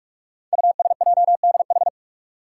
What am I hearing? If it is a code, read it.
Morse code, heard as a keyed tone, about 44 wpm, 700 Hz. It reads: USJDH
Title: Code Groups - 5 Letters - 45wpm